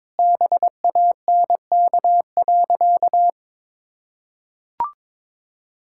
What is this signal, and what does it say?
Morse code, heard as a keyed tone, about 22 wpm, 700 Hz. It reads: BANK.
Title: Sentences from Top 1000 Words - Review Part 6 - 22wpm